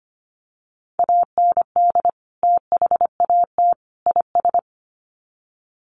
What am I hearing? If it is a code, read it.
Morse code, heard as a keyed tone, about 25 wpm, 700 Hz. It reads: AND THAT IS
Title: Sets of 3 Words 25wpm